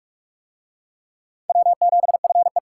Morse code, heard as a keyed tone, about 39 wpm, 700 Hz. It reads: W7FE